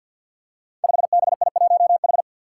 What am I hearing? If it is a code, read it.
Morse code, heard as a keyed tone, about 49 wpm, 700 Hz. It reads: 5BI1H
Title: Five Challenging Letter-Number Combinations 50wpm